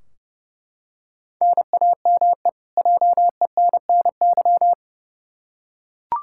Morse code, heard as a keyed tone, about 30 wpm, 700 Hz. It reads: NAME JENNY
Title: K1USN SST QSO Practice 30wpm